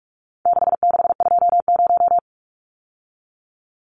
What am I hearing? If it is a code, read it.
Morse code, heard as a keyed tone, about 44 wpm, 700 Hz. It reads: 6620
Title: Numbers - 4-Digit 45wpm